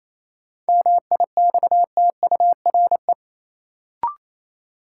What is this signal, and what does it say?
Morse code, heard as a keyed tone, about 28 wpm, 700 Hz. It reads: MIXTURE
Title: Sentences from Top 1500 Words - New Word Drill - 28wpm